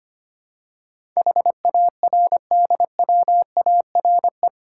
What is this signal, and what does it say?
Morse code, heard as a keyed tone, about 25 wpm, 700 Hz. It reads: HARDWARE
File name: 8 Letter Words - First 3 Letters Given - 25wpm